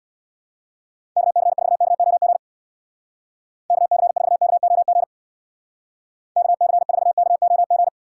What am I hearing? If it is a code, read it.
Morse code, heard as a keyed tone, about 61 wpm, 700 Hz. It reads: KC4KOG KC4KOG KC4KOG
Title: Speed Racing - USA 2x3 Call Signs 50wpm